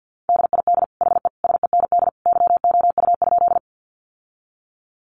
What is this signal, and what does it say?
Morse code, heard as a keyed tone, about 48 wpm, 700 Hz. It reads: DID HE SEND YOU?
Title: Sentences from Top 1500 Words 50wpm